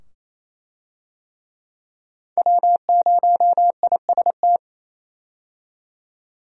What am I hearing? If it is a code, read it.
Morse code, heard as a keyed tone, about 28 wpm, 700 Hz. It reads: W0IST
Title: USA Call Signs - 5 Characters - 28wpm